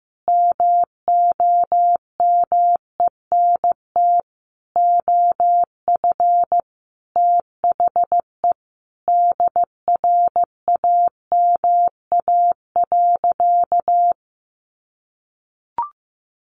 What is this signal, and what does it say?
Morse code, heard as a keyed tone, about 15 wpm, 700 Hz. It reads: MOMENT OF THE DRAMA.
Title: Sentences from Top 1500 Words 15wpm